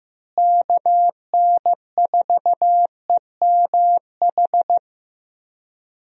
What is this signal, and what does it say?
Morse code, heard as a keyed tone, about 14 wpm, 700 Hz. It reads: KN4EMH
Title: USA Call Signs - 6 Characters - Repeated 3x - 15wpm